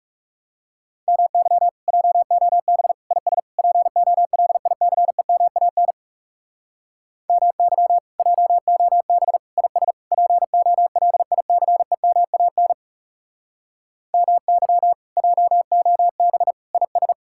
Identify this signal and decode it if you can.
Morse code, heard as a keyed tone, about 37 wpm, 700 Hz. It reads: MY JOB IS POLICEMAN MY JOB IS POLICEMAN MY JOB IS
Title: Speed-Racing Common QSO Phrases - 30wpm